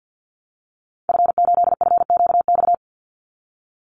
Morse code, heard as a keyed tone, about 50 wpm, 700 Hz. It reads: F8FQX
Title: Call Signs Repeated 3x - 50wpm